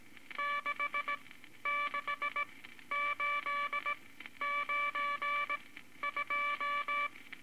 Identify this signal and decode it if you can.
Morse code, heard as a keyed tone, about 19 wpm, spaced as for 12 wpm, 1160 Hz, with noise. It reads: 66892
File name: Morse Code Records / 18 B LCM 165 Ten Words Per Minute